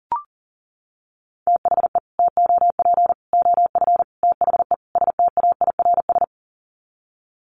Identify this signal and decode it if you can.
Morse code, heard as a keyed tone, about 40 wpm, 700 Hz. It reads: THE TOP OF THE STAIRS
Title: Sets of 5 Words Encore 40wpm